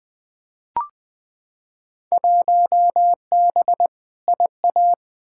Morse code, heard as a keyed tone, about 20 wpm, 700 Hz. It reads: 1B IA
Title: Field Day Exchanges Encore - 20wpm